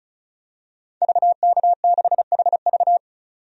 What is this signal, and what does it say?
Morse code, heard as a keyed tone, about 36 wpm, 700 Hz. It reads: VK6HV
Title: DXCC-100 Call Signs - 5 Characters - 35wpm